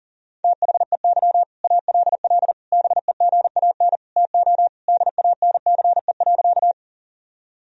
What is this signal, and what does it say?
Morse code, heard as a keyed tone, about 40 wpm, 700 Hz. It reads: THEY ALL BEGAN TO DANCE.
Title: Sentences from Top 1000 Words - Review Part 4 - 40wpm